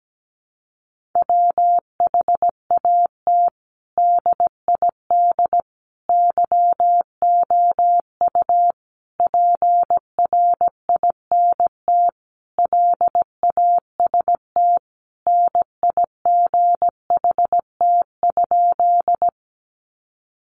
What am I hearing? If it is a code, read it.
Morse code, heard as a keyed tone, about 17 wpm, 700 Hz. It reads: WHAT DID YOU PRINT LAST NIGHT?
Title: Sentences from Top 1400 Words 17wpm